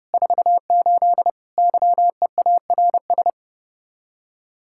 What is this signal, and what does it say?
Morse code, heard as a keyed tone, about 35 wpm, 700 Hz. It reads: 48 YEARS
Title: Speed-Racing Common QSO Phrases - 30wpm